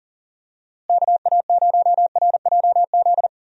Morse code, heard as a keyed tone, about 37 wpm, 700 Hz. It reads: KA0RJZ
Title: USA Call Signs - 6 Characters - Repeated 3x - 40wpm